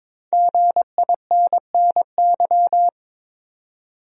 Morse code, heard as a keyed tone, about 21 wpm, 700 Hz. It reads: GINNY
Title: USA Common Names - 22wpm